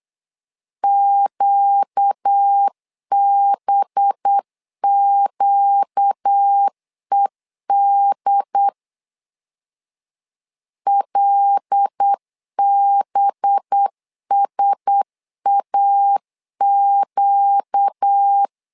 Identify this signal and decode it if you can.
Morse code, heard as a keyed tone, about 8 wpm, 800 Hz. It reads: QBQED LBSAQ
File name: LEZIONI / 5 LBVC / 5 LB 8wpm 2816